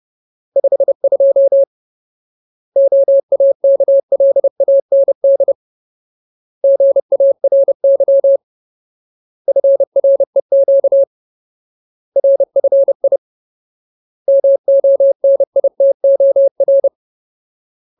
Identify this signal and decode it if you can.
Morse code, heard as a keyed tone, about 30 wpm, 550 Hz. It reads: 52 OAKLAND GARY FREQ RFI MONITOR
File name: Ultimate Word List / 30WPM Ultimate